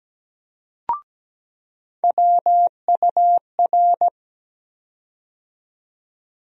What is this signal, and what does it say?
Morse code, heard as a keyed tone, about 17 wpm, 700 Hz. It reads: WUR
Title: USA Call Sign Postfixes - 3 Letters 17wpm